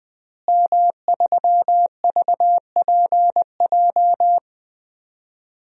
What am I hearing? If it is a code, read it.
Morse code, heard as a keyed tone, about 20 wpm, 700 Hz. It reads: M3VPJ
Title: DXCC-100 Call Signs - 5 Characters - Repeated 2x - 20wpm